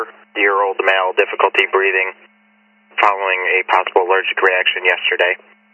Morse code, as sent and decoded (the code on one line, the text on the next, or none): none